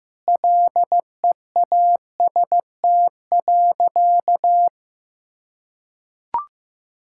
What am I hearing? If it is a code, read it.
Morse code, heard as a keyed tone, about 15 wpm, 700 Hz. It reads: LEAST.
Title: Sentences from Top 1000 Words - Review Part 5 - 15wpm